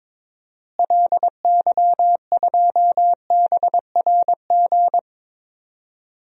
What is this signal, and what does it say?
Morse code, heard as a keyed tone, about 21 wpm, 700 Hz. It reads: LY2BRG
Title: DXCC-100 Call Signs - 6 Characters - Repeated 3x - 22wpm